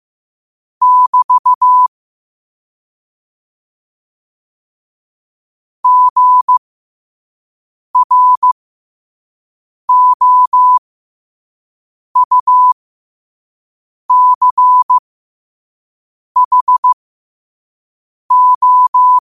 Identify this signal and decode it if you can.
Morse code, heard as a keyed tone, about 15 wpm, spaced as for 5 wpm, 1000 Hz. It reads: <BT> GROUCHO
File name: qotdfiles / 5wpm